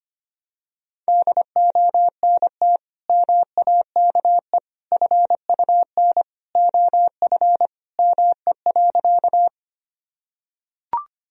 Morse code, heard as a keyed tone, about 25 wpm, 700 Hz. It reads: DONT MAKE FUN OF ME.
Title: Sentences from Top 1000 Words - Review Part 4 - 25wpm